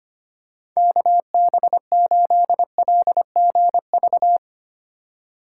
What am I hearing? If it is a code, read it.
Morse code, heard as a keyed tone, about 25 wpm, 700 Hz. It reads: KB8LGV
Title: USA Call Signs - 6 Characters - 2x Character Spacing - 25wpm